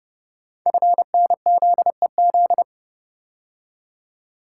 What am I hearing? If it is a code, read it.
Morse code, heard as a keyed tone, about 30 wpm, 700 Hz. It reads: FNZEZ
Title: Code Groups - 5 Letters - 30wpm